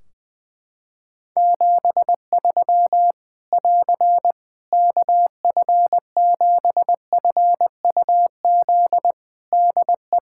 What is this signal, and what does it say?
Morse code, heard as a keyed tone, about 20 wpm, 700 Hz. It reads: 73 <AR> KF7FUZ DE
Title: Practice QSO - Exchange 4 - Ending - 20wpm